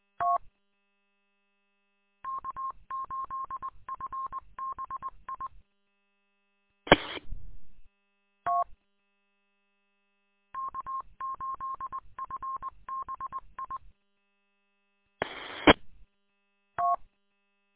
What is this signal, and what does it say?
Morse code, heard as a keyed tone, about 20 wpm, 1070 Hz. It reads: T K8FBI E T K8FBI EE T
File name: k8fbi-43732-combined